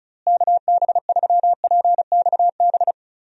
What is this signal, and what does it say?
Morse code, heard as a keyed tone, about 34 wpm, 700 Hz. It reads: KB3PXB
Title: USA Call Signs - 6 Characters - Repeated 2x - 35wpm